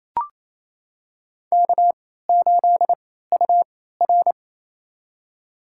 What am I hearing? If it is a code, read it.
Morse code, heard as a keyed tone, about 28 wpm, spaced as for 16 wpm, 700 Hz. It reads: K8UR
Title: USA Call Signs - 4 Characters - 3x Character Spacing - 28wpm